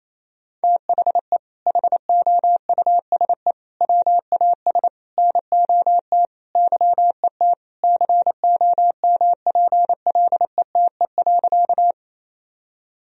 Morse code, heard as a keyed tone, about 28 wpm, 700 Hz. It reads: THE HOUSE WAS NOT YET COMPLETE.
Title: Sentences from Top 1000 Words - Review Part 2 - 28wpm